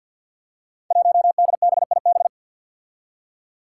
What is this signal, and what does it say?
Morse code, heard as a keyed tone, about 50 wpm, 700 Hz. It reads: 1DBIB
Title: Five Challenging Letter-Number Combinations 50wpm